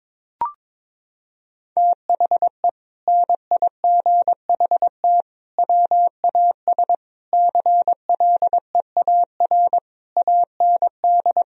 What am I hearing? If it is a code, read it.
Morse code, heard as a keyed tone, about 22 wpm, 700 Hz. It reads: THE NIGHT WAS CLEAR AND
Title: Sentences from Top 1000 Words - Review Part 2 - 22wpm